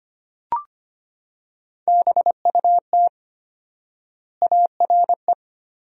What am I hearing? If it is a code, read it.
Morse code, heard as a keyed tone, about 25 wpm, 700 Hz. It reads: BUT ARE
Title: 4x Word-Spacing - Sentences from Top 300 Words - 25wpm